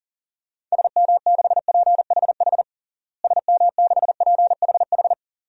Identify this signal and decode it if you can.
Morse code, heard as a keyed tone, about 40 wpm, 700 Hz. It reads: SM6PHH SM6PHH
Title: DXCC-100 Call Signs - 6 Characters - Repeated 2x - 40wpm